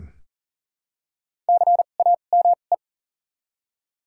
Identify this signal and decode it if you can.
Morse code, heard as a keyed tone, about 40 wpm, spaced as for 29 wpm, 700 Hz. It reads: CAME
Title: 2x Character Spacing - Top 200 Words - 40wpm